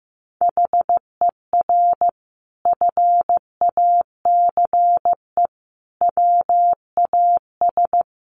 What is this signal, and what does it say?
Morse code, heard as a keyed tone, about 15 wpm, 700 Hz. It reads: HER FACE WAS
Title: Sentences from Top 1200 Words 15wpm